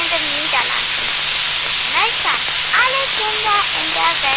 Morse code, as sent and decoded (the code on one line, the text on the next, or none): none